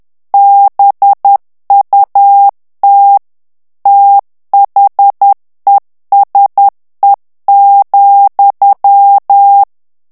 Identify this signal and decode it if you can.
Morse code, heard as a keyed tone, about 10 wpm, 800 Hz. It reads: BUT THESE,
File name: morse mp3 / ANGLAIS texte clair 2 vitesse 10 mots minute